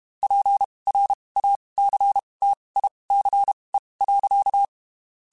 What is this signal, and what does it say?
Morse code, heard as a keyed tone, about 32 wpm, spaced as for 23 wpm, 800 Hz. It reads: PRACTICE.
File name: (spaced) / QSOs Spaced-mp3s / WD0FBT @30 sp